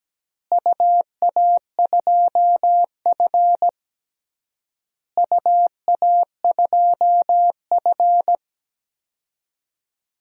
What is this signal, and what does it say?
Morse code, heard as a keyed tone, about 17 wpm, 700 Hz. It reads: UA2F UA2F
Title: DXCC-100 Call Signs - 4 Characters - Repeated 3x - 17wpm